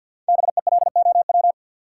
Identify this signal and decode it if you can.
Morse code, heard as a keyed tone, about 48 wpm, 700 Hz. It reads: BELOW
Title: Sentences from Top 1000 Words - Review Part 5 - New Word Drill - 50wpm